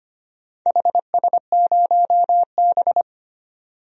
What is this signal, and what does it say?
Morse code, heard as a keyed tone, about 25 wpm, 700 Hz. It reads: HS0B
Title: DXCC-100 Call Signs - 4 Characters - 25wpm